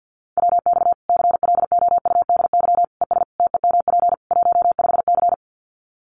Morse code, heard as a keyed tone, about 48 wpm, 700 Hz. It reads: WX CLOUDY ES TEMP 15C
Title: Common QSO Phrases 50wpm